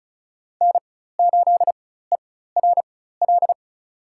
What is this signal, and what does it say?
Morse code, heard as a keyed tone, about 36 wpm, spaced as for 16 wpm, 700 Hz. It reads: N8ERL
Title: USA Call Signs - 5 Characters - 4x Character Spacing - 35wpm